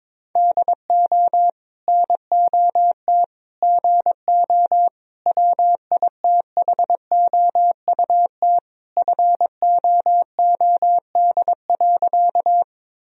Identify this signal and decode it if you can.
Morse code, heard as a keyed tone, about 22 wpm, 690 Hz. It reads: DO NOT GO WITHOUT FOOD.
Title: Sentences from Top 1000 Words - Review Part 3 - 22wpm